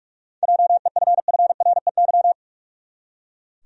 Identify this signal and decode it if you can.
Morse code, heard as a keyed tone, about 46 wpm, 700 Hz. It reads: JEFFREY
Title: USA Common Names - 45wpm